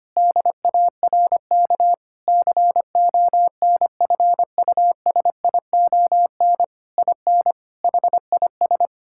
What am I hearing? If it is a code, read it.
Morse code, heard as a keyed tone, about 25 wpm, 700 Hz. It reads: DARK CONFUSION IN HIS
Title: Sentences from Top 1600 Words 25wpm